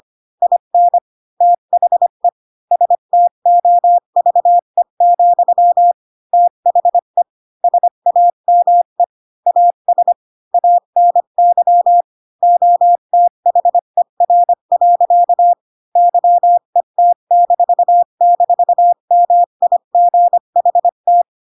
Morse code, heard as a keyed tone, about 25 wpm, 700 Hz. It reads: IN THE STOVE, THE SAME AS ANY OTHER. YET--MIGHT
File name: Pinocchio / Pinocchio ch1 20 25 ik1hge